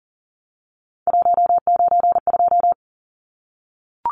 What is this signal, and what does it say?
Morse code, heard as a keyed tone, about 38 wpm, 700 Hz. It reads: 192
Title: Numbers - 3-Digit 40wpm